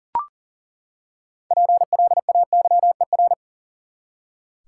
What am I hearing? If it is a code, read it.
Morse code, heard as a keyed tone, about 40 wpm, 700 Hz. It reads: PLAYER
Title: Six Letter Words - 40wpm